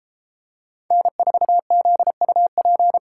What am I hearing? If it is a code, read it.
Morse code, heard as a keyed tone, about 32 wpm, 700 Hz. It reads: N4ZUP